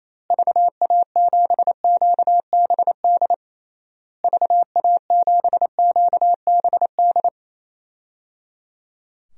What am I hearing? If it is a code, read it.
Morse code, heard as a keyed tone, about 28 wpm, 700 Hz. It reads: VA7QBD VA7QBD
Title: DXCC-100 Call Signs - 6 Characters - Repeated 3x - 28wpm